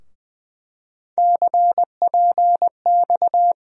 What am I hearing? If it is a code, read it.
Morse code, heard as a keyed tone, about 21 wpm, 700 Hz. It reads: CPX
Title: USA Call Sign Postfixes - 3 Letters 20wpm